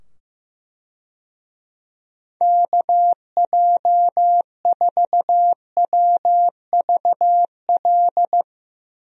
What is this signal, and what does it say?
Morse code, heard as a keyed tone, about 15 wpm, 700 Hz. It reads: KJ4WVL
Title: USA Call Signs - 6 Characters - 15wpm